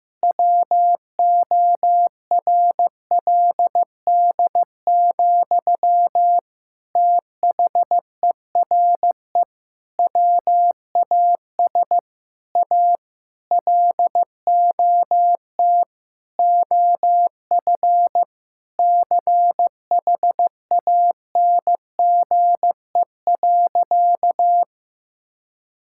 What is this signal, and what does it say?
Morse code, heard as a keyed tone, about 15 wpm, 700 Hz. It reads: WORLD, THERE WAS A LOT OF CHANGE.
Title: Sentences from Top 1000 Words - Review Part 5 - 15wpm